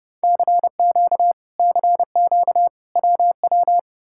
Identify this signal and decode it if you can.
Morse code, heard as a keyed tone, about 35 wpm, 700 Hz. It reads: CQ CQ WW